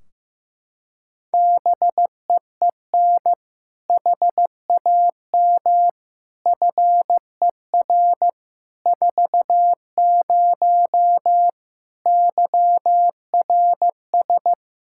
Morse code, heard as a keyed tone, about 15 wpm, 700 Hz. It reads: BEEN HAM FER 40 YRS
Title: Practice QSO - Exchange 3 - More Info - 15wpm